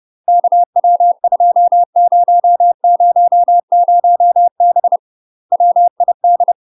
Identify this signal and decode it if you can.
Morse code, heard as a keyed tone, about 30 wpm, 700 Hz. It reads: KW2000B WID
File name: UK MASTER QSOs TRAINING / 30WPM / UK MASTER QSOs 0019